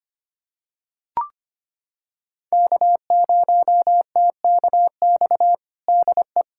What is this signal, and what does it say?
Morse code, heard as a keyed tone, about 25 wpm, 700 Hz. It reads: K0TKX DE
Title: USA Callsign DE Callsign - 5 Characters - 25wpm